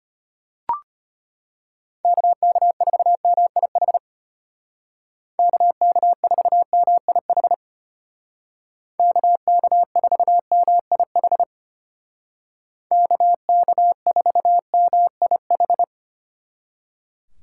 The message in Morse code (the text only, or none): KK4MIH KK4MIH KK4MIH KK4MIH